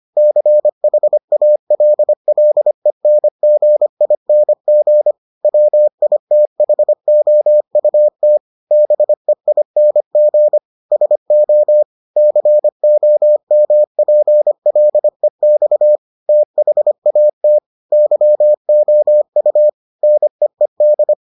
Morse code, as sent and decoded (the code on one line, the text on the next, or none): -.-. .... .- .-.. .-.. -. --. .. -. --. / .-- .. - .... --- ..- - / -... .. -. --. / ... --- / -.-. --- -- .--. .-.. -..- / - .... .- - / -.-- --- ..- / -. . -..
CHALLENGING WITHOUT BEING SO COMPLEX THAT YOU NEED